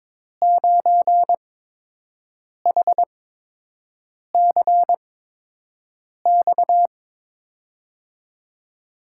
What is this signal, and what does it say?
Morse code, heard as a keyed tone, about 22 wpm, spaced as for 6 wpm, 700 Hz. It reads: 9HCX
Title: USA Call Signs - 5 Characters - 8x Character Spacing - 22wpm